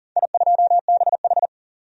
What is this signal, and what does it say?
Morse code, heard as a keyed tone, about 40 wpm, 700 Hz. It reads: I2BH
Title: DXCC-100 Call Signs - 4 Characters - 40wpm